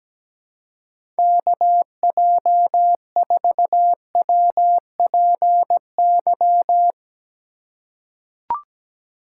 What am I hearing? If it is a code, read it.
Morse code, heard as a keyed tone, about 17 wpm, 700 Hz. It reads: KJ4WPY
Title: USA Call Signs - 6 Characters - 3x Character Spacing - 17wpm